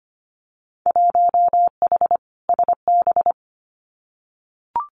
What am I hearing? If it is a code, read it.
Morse code, heard as a keyed tone, about 25 wpm, 700 Hz. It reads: 1H SB
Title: Winter Field Day Exchanges 25wpm